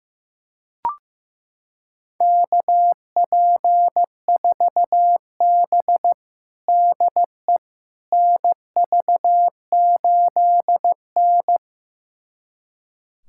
Morse code, heard as a keyed tone, about 15 wpm, 700 Hz. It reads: KP4B DE NV8N
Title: USA Callsign DE Callsign - 4 Characters - 15wpm